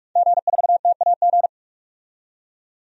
Morse code, heard as a keyed tone, about 46 wpm, 700 Hz. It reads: G4TAG
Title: DXCC-100 Call Signs - 5 Characters - 45wpm